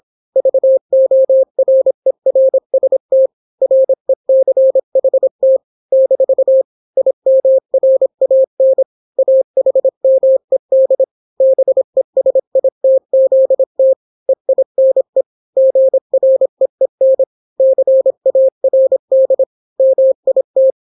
Morse code, heard as a keyed tone, about 26 wpm, 530 Hz. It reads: VORERST RECHT - IMRAN AHMED BESITZT EINE GREEN CARD MIT